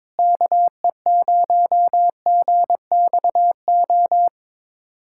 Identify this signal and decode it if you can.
Morse code, heard as a keyed tone, about 22 wpm, 700 Hz. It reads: KE0GXO
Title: USA Callsign DE Callsign - 6 Characters - 22wpm